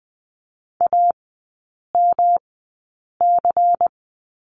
Morse code, heard as a keyed tone, about 20 wpm, spaced as for 8 wpm, 700 Hz. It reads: AMC
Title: 127 Lesson 21e - Number 6 - Three Characters 20wpm